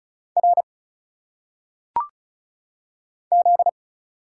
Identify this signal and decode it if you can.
Morse code, heard as a keyed tone, about 34 wpm, 700 Hz. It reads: R Z
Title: Single Letters - Phonetic Pronunciation 35wpm